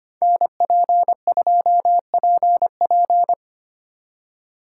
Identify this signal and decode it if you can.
Morse code, heard as a keyed tone, about 30 wpm, 700 Hz. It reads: NP2PP
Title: Speed Racing - USA 2x2 Call Signs 25wpm